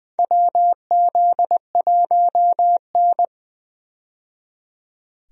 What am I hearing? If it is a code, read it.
Morse code, heard as a keyed tone, about 20 wpm, 700 Hz. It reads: WZ1N
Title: USA Call Signs - 4 Characters - 20wpm